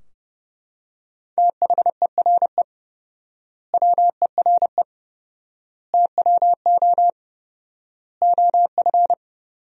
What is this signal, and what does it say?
Morse code, heard as a keyed tone, about 30 wpm, 700 Hz. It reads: THERE WERE TWO OF
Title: 4x Word-Spacing - Sentences from Top 100 Words - 30wpm